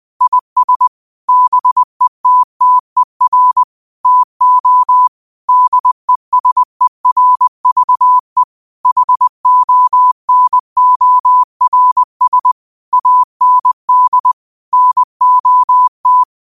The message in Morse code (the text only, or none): IS BETTER TO DESERVE HONORS AND NOT